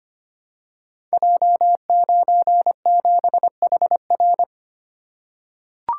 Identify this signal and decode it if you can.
Morse code, heard as a keyed tone, about 25 wpm, 700 Hz. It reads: J97HR
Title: Code Groups - 5 Letter:Numbers - 25wpm